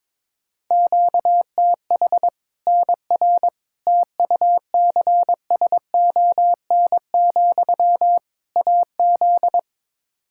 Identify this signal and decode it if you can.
Morse code, heard as a keyed tone, about 21 wpm, 700 Hz. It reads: QTH NR TUCSON, AZ
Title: K1USN SST QSO Practice 22wpm